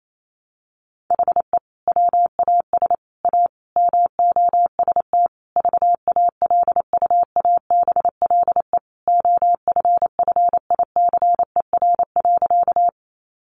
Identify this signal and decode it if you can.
Morse code, heard as a keyed tone, about 28 wpm, 700 Hz. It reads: HE WAS A MOST VALUABLE OFFICER.
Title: Sentences from Top 1200 Words 28wpm